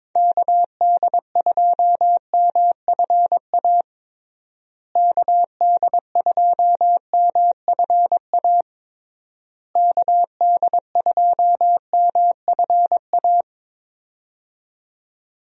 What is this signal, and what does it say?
Morse code, heard as a keyed tone, about 22 wpm, 700 Hz. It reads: KD2MFA KD2MFA KD2MFA
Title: USA Call Signs - 6 Characters - Repeated 3x - 22wpm